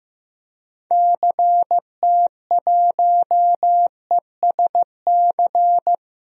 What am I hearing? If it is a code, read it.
Morse code, heard as a keyed tone, about 16 wpm, 700 Hz. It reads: CT1ESC
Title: DXCC-100 Call Signs - 6 Characters - Repeated 2x - 15wpm